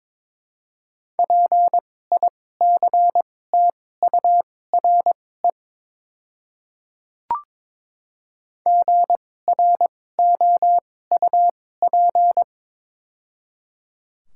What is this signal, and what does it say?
Morse code, heard as a keyed tone, about 22 wpm, spaced as for 16 wpm, 700 Hz. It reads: PICTURE GROUP
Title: 2x Character Spacing - Top 300 Words - 22wpm